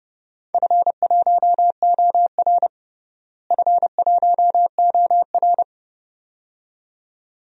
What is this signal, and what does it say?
Morse code, heard as a keyed tone, about 30 wpm, 700 Hz. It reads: F1OR F1OR
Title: DXCC-100 Call Signs - 4 Characters - Repeated 2x - 30wpm